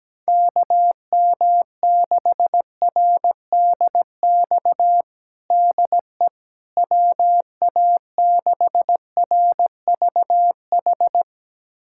Morse code, heard as a keyed tone, about 17 wpm, 700 Hz. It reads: KM6RDX DE WA6RVH
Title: USA Callsign DE Callsign - 6 Characters - 17wpm